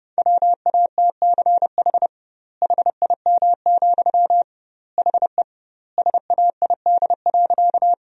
Morse code, heard as a keyed tone, about 30 wpm, 700 Hz. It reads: WATCH HIM, HE SAID.
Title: 2x Word-Spacing - Sentences from Top 300 Words - 30wpm